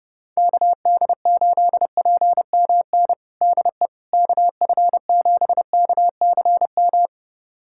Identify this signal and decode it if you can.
Morse code, heard as a keyed tone, about 30 wpm, 700 Hz. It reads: KD8PMN DE KF7KCM
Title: USA Callsign DE Callsign - 6 Characters - 30wpm